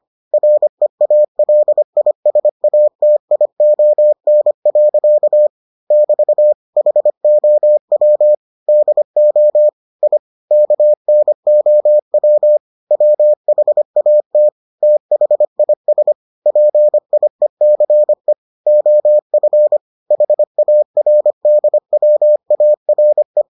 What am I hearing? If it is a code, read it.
Morse code, heard as a keyed tone, about 25 wpm, 600 Hz. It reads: REALISATION. <BT> HOW DO I KNOW WHAT THIS PIECE OF HARDWARE